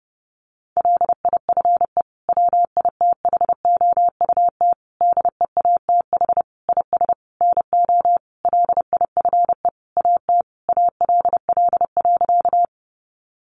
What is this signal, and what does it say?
Morse code, heard as a keyed tone, about 30 wpm, 700 Hz. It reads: LIFE WITHOUT DEATH IS NO LIFE AT ALL.
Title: Sentences from Top 1100 Words 30wpm